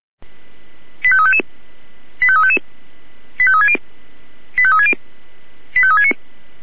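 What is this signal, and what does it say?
Five-tone selective call (ZVEI), heard as quick runs of five tones, tones of about 70 ms. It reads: 85355 85355 85360 85360 85369
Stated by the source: ZVEI five-tone selective call 85369, received by SDR.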